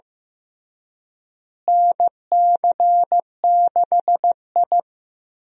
Morse code, heard as a keyed tone, about 15 wpm, 700 Hz. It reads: NC6I
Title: USA Call Signs - 4 Characters - Repeated 2x - 15wpm